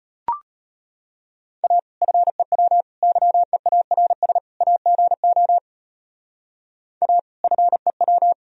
Morse code, heard as a keyed tone, about 31 wpm, 700 Hz. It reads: A FEW YEARS AGO A FEW